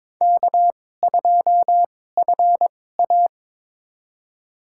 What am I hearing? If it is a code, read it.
Morse code, heard as a keyed tone, about 22 wpm, spaced as for 16 wpm, 700 Hz. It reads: K2FA
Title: USA Call Signs - 4 Characters - 2x Character Spacing - 22wpm